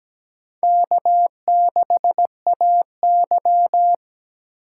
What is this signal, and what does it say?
Morse code, heard as a keyed tone, about 17 wpm, 700 Hz. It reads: K6AY
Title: USA Call Signs - 4 Characters - 3x Character Spacing - 17wpm